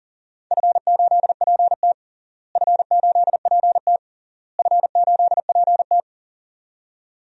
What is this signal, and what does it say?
Morse code, heard as a keyed tone, about 40 wpm, 700 Hz. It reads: F8PT F8PT F8PT
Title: DXCC-100 Call Signs - 4 Characters - Repeated 3x - 40wpm